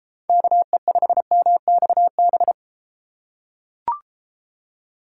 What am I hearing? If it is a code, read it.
Morse code, heard as a keyed tone, about 32 wpm, 700 Hz. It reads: KE5MXB